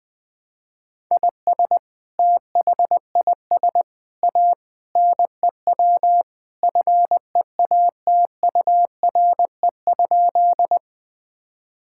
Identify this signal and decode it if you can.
Morse code, heard as a keyed tone, about 20 wpm, 700 Hz. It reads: IS THIS A NEW FEATURE?
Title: Sentences from Top 1000 Words - Review Part 5 - 20wpm